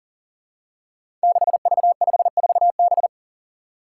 Morse code, heard as a keyed tone, about 42 wpm, 700 Hz. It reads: 6V54B
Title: Five Challenging Letter-Number Combinations 40wpm